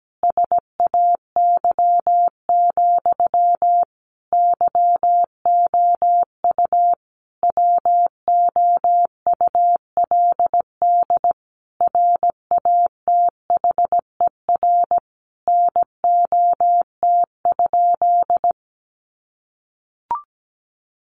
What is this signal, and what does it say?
Morse code, heard as a keyed tone, about 17 wpm, 700 Hz. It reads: SAY, YOU WOULD RATHER NOT?
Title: Sentences from Top 700 Words 17wpm